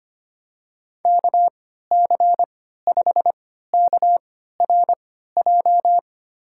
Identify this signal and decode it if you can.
Morse code, heard as a keyed tone, about 25 wpm, spaced as for 14 wpm, 700 Hz. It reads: KC5KRJ
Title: USA Call Signs - 6 Characters - 3x Character Spacing - 25wpm